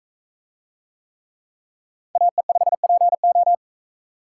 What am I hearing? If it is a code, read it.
Morse code, heard as a keyed tone, about 43 wpm, 700 Hz. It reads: AE5PO